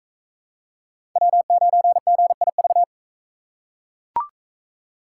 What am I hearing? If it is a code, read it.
Morse code, heard as a keyed tone, about 41 wpm, 700 Hz. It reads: W9GIV